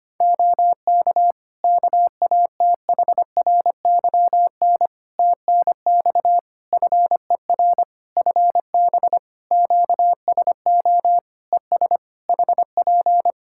Morse code, heard as a keyed tone, about 25 wpm, 700 Hz. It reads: OK KATHRYN TNX FER FB QSO ES HP
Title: K1USN SST QSO Practice 25wpm